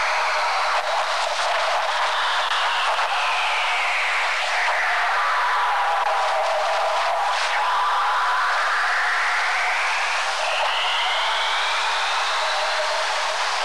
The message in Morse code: none